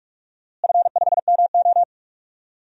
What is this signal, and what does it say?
Morse code, heard as a keyed tone, about 46 wpm, 700 Hz. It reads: F5MO
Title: DXCC-100 Call Signs - 4 Characters - Repeated 2x - 45wpm